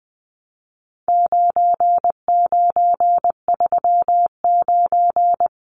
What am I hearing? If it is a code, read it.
Morse code, heard as a keyed tone, about 20 wpm, 700 Hz. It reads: 9939
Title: Numbers - 4-Digit 20wpm